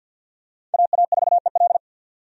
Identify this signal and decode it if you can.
Morse code, heard as a keyed tone, about 51 wpm, 690 Hz. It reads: AA4EL
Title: USA Call Signs - 2x Character Spacing - 50wpm